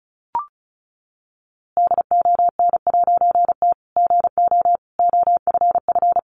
Morse code, heard as a keyed tone, about 34 wpm, 700 Hz. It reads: DON'T GO OFF
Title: A Sound Of Thunder 35wpm